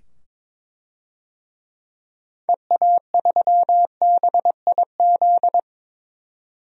Morse code, heard as a keyed tone, about 21 wpm, 700 Hz. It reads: EA3BIZ
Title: DXCC-100 Call Signs - 6 Characters - 22wpm